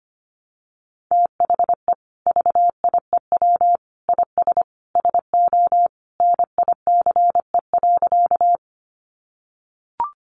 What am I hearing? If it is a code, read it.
Morse code, heard as a keyed tone, about 25 wpm, 700 Hz. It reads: THE VIEW IS SO NICE.
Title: Additional Easy Phrases 25wpm